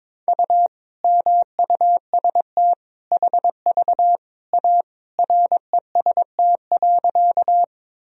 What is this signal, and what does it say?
Morse code, heard as a keyed tone, about 22 wpm, 700 Hz. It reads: U MUST HV A REST.
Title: Sentences from Top 500 Words and Abbreviations 22wpm